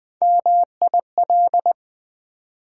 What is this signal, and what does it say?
Morse code, heard as a keyed tone, about 20 wpm, 700 Hz. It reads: MIL
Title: Build A Word - 1 Letter At A Time - 4-Letter Words - 20wpm